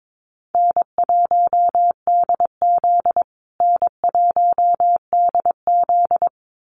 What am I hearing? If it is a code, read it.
Morse code, heard as a keyed tone, about 22 wpm, 700 Hz. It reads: N1DZ N1DZ
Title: USA Call Signs Repeated 2x 22wpm